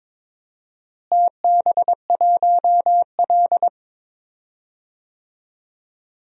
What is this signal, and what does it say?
Morse code, heard as a keyed tone, about 22 wpm, 700 Hz. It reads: TB1L
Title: DXCC-100 Call Signs - 4 Characters - Repeated 2x - 22wpm